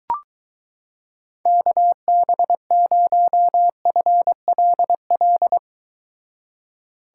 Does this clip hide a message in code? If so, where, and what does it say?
Yes: Morse code from 1.5 to 5.6 s, heard as a keyed tone, about 23 wpm, 700 Hz. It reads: KB0FLL